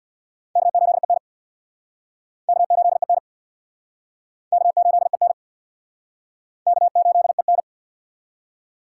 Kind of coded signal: Morse code

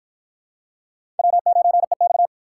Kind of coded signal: Morse code

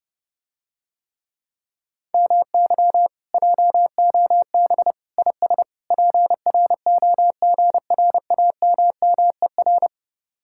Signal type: Morse code

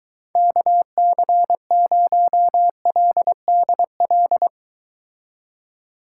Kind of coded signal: Morse code

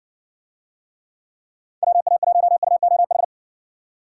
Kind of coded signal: Morse code

